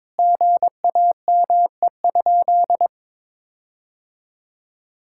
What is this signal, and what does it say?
Morse code, heard as a keyed tone, about 22 wpm, 700 Hz. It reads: GAME?
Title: Sentences from Top 1600 Words 22wpm